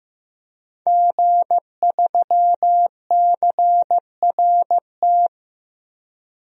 Morse code, heard as a keyed tone, about 15 wpm, 700 Hz. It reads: G3CRT
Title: DXCC-100 Call Signs - 5 Characters - Repeated 2x - 15wpm